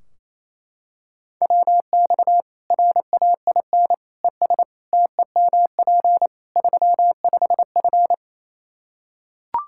Morse code, heard as a keyed tone, about 28 wpm, 700 Hz. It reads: WX RAIN ES TEMP 35F
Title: Practice QSO - Exchange 2 - Info - 28wpm